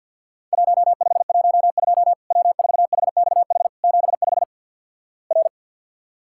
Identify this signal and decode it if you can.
Morse code, heard as a keyed tone, about 50 wpm, 700 Hz. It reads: 1512 W4HXH 75 R
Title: Worked All Europe DX Contest - QTC Exchange - 50wpm